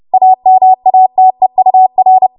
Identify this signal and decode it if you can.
Morse code, heard as a keyed tone, about 30 wpm, 750 Hz. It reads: AMATEUR
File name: Archive / 30 WPM / 30WPM